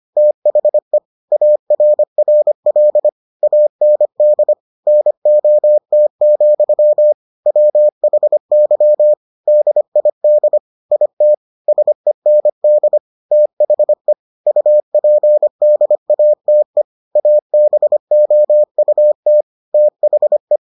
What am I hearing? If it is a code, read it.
Morse code, heard as a keyed tone, about 25 wpm, 600 Hz. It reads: THE ARRL AND NOT, WHY DID IT SEND THE UPDATE ABOUT THE